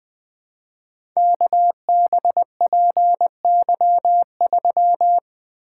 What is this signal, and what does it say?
Morse code, heard as a keyed tone, about 20 wpm, 700 Hz. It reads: KBPY3
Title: Code Groups - 5 Letter:Numbers - 20wpm